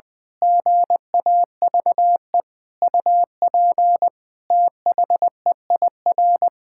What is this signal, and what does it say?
Morse code, heard as a keyed tone, about 20 wpm, 700 Hz. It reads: GAVE UP THEIR
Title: Pangram Challenge v2 - 20wpm